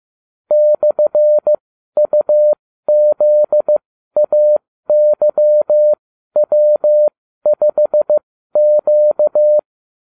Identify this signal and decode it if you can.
Morse code, heard as a keyed tone, about 15 wpm, spaced as for 12 wpm, 600 Hz. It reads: /UZAYW5Q
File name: cw / ubacw29 2